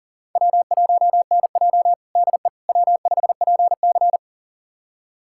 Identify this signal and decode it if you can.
Morse code, heard as a keyed tone, about 37 wpm, 700 Hz. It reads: W1NJ DE W5PC
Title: USA Callsign DE Callsign - 4 Characters - 40wpm